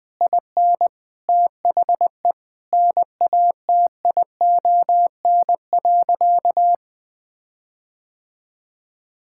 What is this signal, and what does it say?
Morse code, heard as a keyed tone, about 21 wpm, 700 Hz. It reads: IN THE NATION.
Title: Sentences with 7-Word Phrases - 20wpm